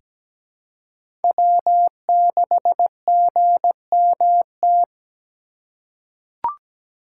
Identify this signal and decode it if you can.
Morse code, heard as a keyed tone, about 17 wpm, 700 Hz. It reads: W6GMT
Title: USA Call Signs - 5 Characters - 2x Character Spacing - 17wpm